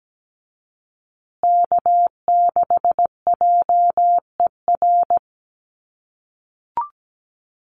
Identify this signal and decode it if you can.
Morse code, heard as a keyed tone, about 17 wpm, 700 Hz. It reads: K6JER
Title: USA Call Signs Repeated 3x 17wpm